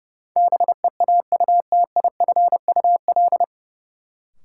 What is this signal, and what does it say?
Morse code, heard as a keyed tone, about 30 wpm, 700 Hz. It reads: BEAUTIFUL
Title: Sentences from Top 900 Words - New Word Drill - 30wpm